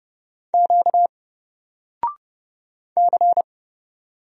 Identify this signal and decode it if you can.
Morse code, heard as a keyed tone, about 30 wpm, 700 Hz. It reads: Q C
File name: Single-Letter-Number-30wpm